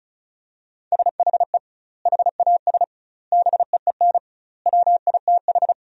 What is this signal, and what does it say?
Morse code, heard as a keyed tone, about 34 wpm, 700 Hz. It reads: SHE HAS BEEN WITH
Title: 2x Word-Spacing - Sentences from Top 200 Words - 35wpm